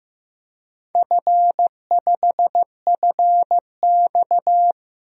Morse code, heard as a keyed tone, about 16 wpm, 700 Hz. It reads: F5FX
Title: DXCC-100 Call Signs - 4 Characters - Repeated 3x - 15wpm